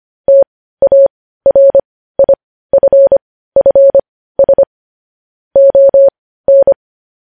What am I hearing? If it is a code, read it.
Morse code, heard as a keyed tone, about 25 wpm, spaced as for 15 wpm, 550 Hz. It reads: TARIFFS ON
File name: podcast / News headlines-1x-25x15